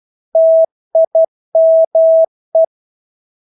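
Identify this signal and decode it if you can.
Morse code, heard as a keyed tone, about 12 wpm, 650 Hz. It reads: TIME
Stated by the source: Morse practice of English words at 12 wpm.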